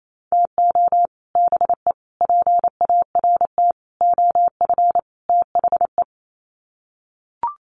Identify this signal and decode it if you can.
Morse code, heard as a keyed tone, about 28 wpm, 700 Hz. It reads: TO BE PART OF THE
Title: Sets of 5 Words 28wpm